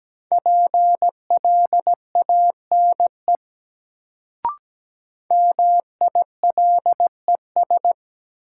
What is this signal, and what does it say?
Morse code, heard as a keyed tone, about 17 wpm, 700 Hz. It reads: PLANE MILES
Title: Sentences from Top 1000 Words - Review Part 3 - New Word Drill - 17wpm